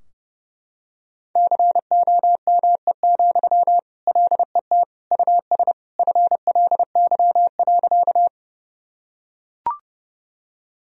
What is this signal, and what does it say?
Morse code, heard as a keyed tone, about 30 wpm, 700 Hz. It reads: COME, LET US FLY.
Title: Sentences from Top 1000 Words - Review Part 2 - 30wpm